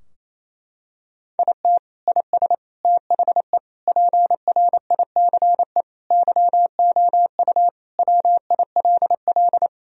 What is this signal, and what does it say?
Morse code, heard as a keyed tone, about 27 wpm, 700 Hz. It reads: IT IS THE PRICE YOU WILL
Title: Sentences from Top 1000 Words - Review Part 4 - 28wpm